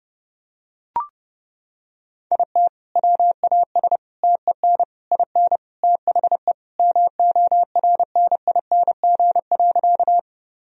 Morse code, heard as a keyed tone, about 30 wpm, 700 Hz. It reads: IT WAS TEN IN THE MORNING.
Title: Sentences from Top 1000 Words - Review Part 2 - 30wpm